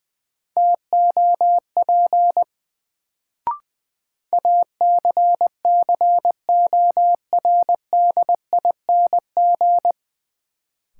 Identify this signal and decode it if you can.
Morse code, heard as a keyed tone, about 20 wpm, 700 Hz. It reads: TOP ACCORDING
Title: Sentences from Top 700 Words - New Word Drill - 20wpm